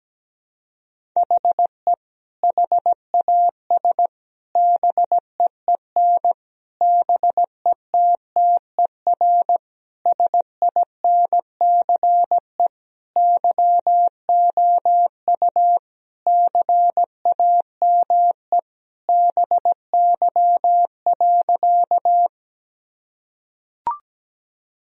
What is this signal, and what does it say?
Morse code, heard as a keyed tone, about 17 wpm, 700 Hz. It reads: HE HAS BEEN BETTER SINCE YOU CAME BY.
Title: Sentences from Top 1000 Words - Review Part 1 - 17wpm